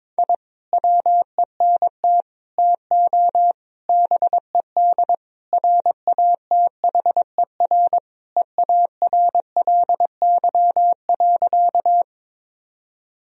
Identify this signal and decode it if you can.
Morse code, heard as a keyed tone, about 22 wpm, 700 Hz. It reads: I WENT TO BED RATHER EARLY.
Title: Sentences from Top 1000 Words - Review Part 4 - 22wpm